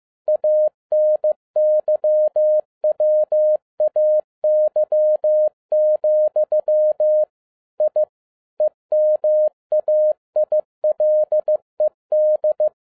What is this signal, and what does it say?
Morse code, heard as a keyed tone, about 15 wpm, 610 Hz. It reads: ANYWAY, I EMAILED
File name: 2011-10-mp3 / n07-A DOT 15wpm.mp3